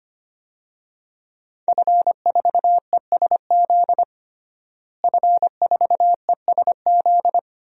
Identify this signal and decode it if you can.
Morse code, heard as a keyed tone, about 25 wpm, 700 Hz. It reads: F4ESZ F4ESZ
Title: DXCC-100 Call Signs - 5 Characters - Repeated 2x - 25wpm